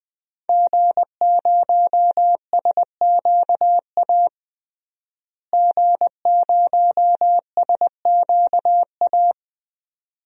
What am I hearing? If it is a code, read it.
Morse code, heard as a keyed tone, about 20 wpm, 700 Hz. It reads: G0SQA G0SQA
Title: DXCC-100 Call Signs - 5 Characters - Repeated 2x - 20wpm